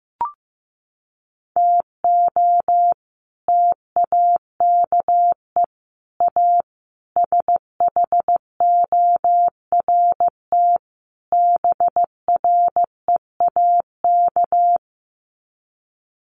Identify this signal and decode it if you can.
Morse code, heard as a keyed tone, about 15 wpm, 700 Hz. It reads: TO TAKE A SHORT BREAK
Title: Sets of 5 Words Encore2 15wpm